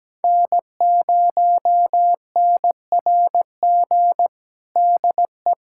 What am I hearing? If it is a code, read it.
Morse code, heard as a keyed tone, about 17 wpm, 700 Hz. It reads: N0NRG DE
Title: USA Callsign DE Callsign 17wpm